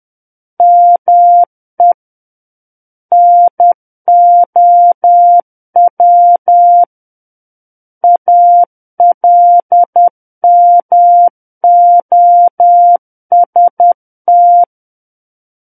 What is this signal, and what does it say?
Morse code, heard as a keyed tone, about 10 wpm, 700 Hz. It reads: ME NOW ALMOST
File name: Book One / 01-Chapter 1 - 10 WPM